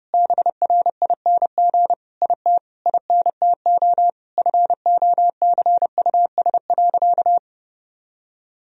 Morse code, heard as a keyed tone, about 32 wpm, 700 Hz. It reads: BRING IT INTO FOCUS.
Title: Sentences from Top 1000 Words - Review Part 4 - 30wpm